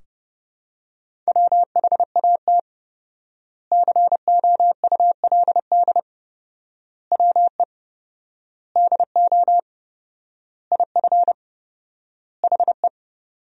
Morse code, heard as a keyed tone, about 31 wpm, 700 Hz. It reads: WHAT COULD WE DO IF HE
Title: 4x Word-Spacing - Sentences from Top 400 Words - 30wpm